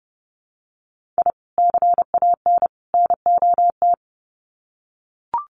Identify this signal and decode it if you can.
Morse code, heard as a keyed tone, about 30 wpm, 700 Hz. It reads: I CAN NOT
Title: Sets of 3 Words 30wpm